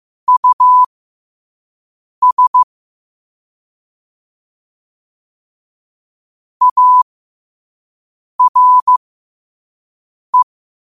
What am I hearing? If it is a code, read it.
Morse code, heard as a keyed tone, about 15 wpm, spaced as for 5 wpm, 1000 Hz. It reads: US ARE
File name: qotdfiles / 5wpm